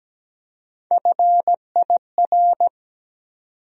Morse code, heard as a keyed tone, about 17 wpm, 700 Hz. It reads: FIR
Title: USA Call Sign Postfixes - 3 Letters 17wpm